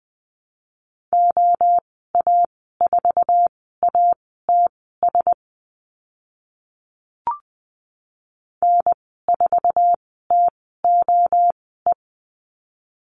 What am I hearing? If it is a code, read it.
Morse code, heard as a keyed tone, about 20 wpm, spaced as for 15 wpm, 700 Hz. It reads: OA4ATS N4TOE
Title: 062 Lesson 11g - Letter h - Call Signs 20wpm